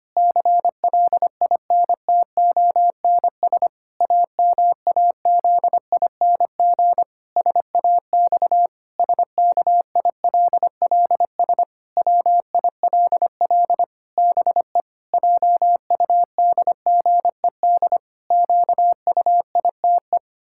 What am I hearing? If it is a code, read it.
Morse code, heard as a keyed tone, about 25 wpm, 700 Hz. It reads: CLINTONS AMAZING SAX SKILLS WILL BE JUDGED QUITE
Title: Pangram Challenge v2 - 25wpm